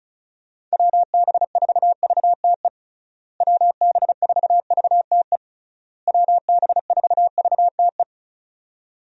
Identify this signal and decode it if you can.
Morse code, heard as a keyed tone, about 34 wpm, 700 Hz. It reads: WB4VTE WB4VTE WB4VTE
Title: USA Call Signs - 6 Characters - Repeated 3x - 35wpm